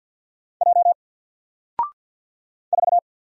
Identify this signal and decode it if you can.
Morse code, heard as a keyed tone, about 49 wpm, 700 Hz. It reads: J 4
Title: Single Letter-Number 50wpm